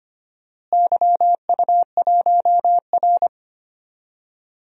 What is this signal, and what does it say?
Morse code, heard as a keyed tone, about 25 wpm, 700 Hz. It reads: YU1R
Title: DXCC-100 Call Signs - 4 Characters - Repeated 2x - 25wpm